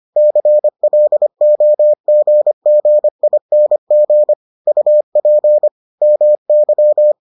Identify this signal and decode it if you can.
Morse code, heard as a keyed tone, about 25 wpm, 600 Hz. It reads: CLOGGING UP MY